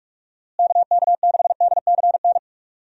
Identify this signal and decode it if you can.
Morse code, heard as a keyed tone, about 46 wpm, 700 Hz. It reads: KK6DCN
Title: USA Call Signs - 6 Characters - Repeated 3x - 45wpm